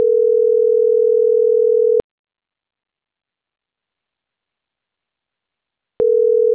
Ringback tone (North America), 440 + 480 Hz.